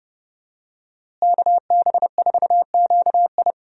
Morse code, heard as a keyed tone, about 30 wpm, 700 Hz. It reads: KB4QI
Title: USA Call Signs - 5 Characters - Repeated 3x - 30wpm